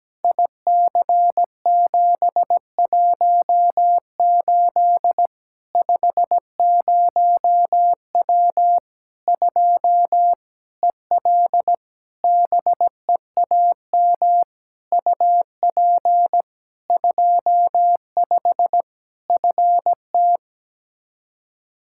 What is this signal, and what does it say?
Morse code, heard as a keyed tone, about 17 wpm, 700 Hz. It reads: IC718 50W 2 EL BEAM UP 25 FT
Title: K1USN SST QSO Practice 17wpm